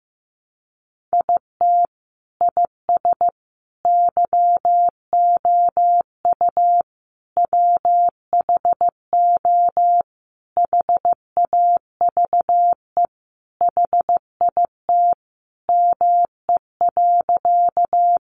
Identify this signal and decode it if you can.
Morse code, heard as a keyed tone, about 15 wpm, 700 Hz. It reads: IT IS YOU WHO HAVE HIT ME.
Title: Sentences from Top 1200 Words 15wpm